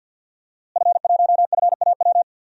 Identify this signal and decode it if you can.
Morse code, heard as a keyed tone, about 47 wpm, 700 Hz. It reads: F1FAW
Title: DXCC-100 Call Signs - 5 Characters - Repeated 3x - 50wpm